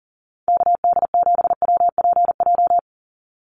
Morse code, heard as a keyed tone, about 40 wpm, 700 Hz. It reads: KD7WPJ
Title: NA SOTA Activator Callsigns 40wpm